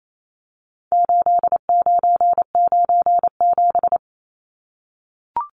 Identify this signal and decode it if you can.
Morse code, heard as a keyed tone, about 28 wpm, 700 Hz. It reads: 8997
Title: Numbers - 4-Digit 28wpm